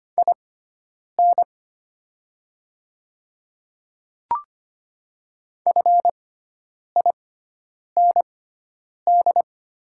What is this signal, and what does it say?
Morse code, heard as a keyed tone, about 25 wpm, spaced as for 9 wpm, 700 Hz. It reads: IN FIND
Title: 6x Character Spacing - Top 100 Words - 25wpm